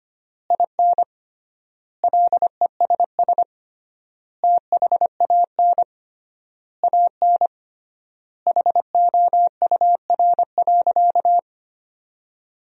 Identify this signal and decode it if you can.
Morse code, heard as a keyed tone, about 25 wpm, 700 Hz. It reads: IN LESS THAN AN HOUR.
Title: 3x Word-Spacing - Sentences from Top 500 Words - 25wpm